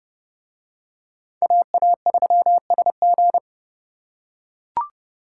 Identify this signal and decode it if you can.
Morse code, heard as a keyed tone, about 29 wpm, 700 Hz. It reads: AA3SG